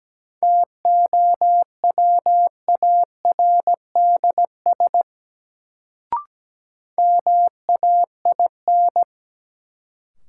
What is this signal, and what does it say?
Morse code, heard as a keyed tone, about 17 wpm, 700 Hz. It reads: TOWARDS MAIN
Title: Sentences from Top 1000 Words - Review Part 2 - New Word Drill - 17wpm